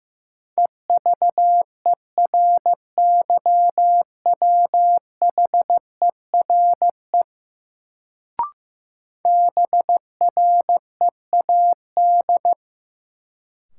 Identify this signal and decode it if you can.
Morse code, heard as a keyed tone, about 15 wpm, 700 Hz. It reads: EVERYWHERE BREAD
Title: Sentences from Top 1600 Words - New Word Drill Alt - 15wpm